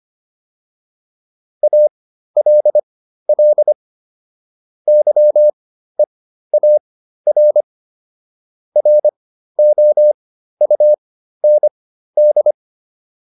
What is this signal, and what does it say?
Morse code, heard as a keyed tone, about 25 wpm, spaced as for 13 wpm, 600 Hz. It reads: ALL YEAR ROUND